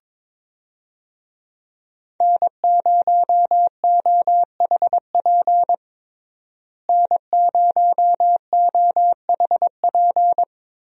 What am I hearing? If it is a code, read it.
Morse code, heard as a keyed tone, about 22 wpm, 700 Hz. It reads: N0OHP N0OHP
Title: USA Call Signs - 5 Characters - Repeated 3x - 22wpm